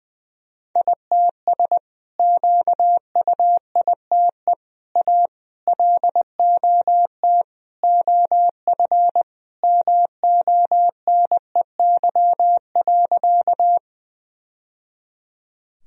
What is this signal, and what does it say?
Morse code, heard as a keyed tone, about 20 wpm, 700 Hz. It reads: ITS QUITE A LOT OF MONEY.
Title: Sentences from Top 1000 Words - Review Part 3 - 20wpm